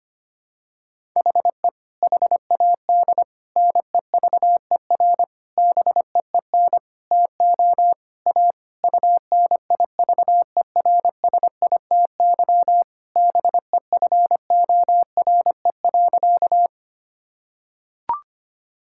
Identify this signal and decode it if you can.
Morse code, heard as a keyed tone, about 25 wpm, 700 Hz. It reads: HE HAD NEVER BEEN TO A UNIVERSITY BEFORE.
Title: Sentences from Top 1000 Words - Review Part 5 - 25wpm